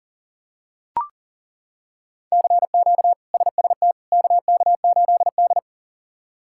Morse code, heard as a keyed tone, about 40 wpm, 700 Hz. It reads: CQ SST KK8D
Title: K1USN SST QSO Practice - 40wpm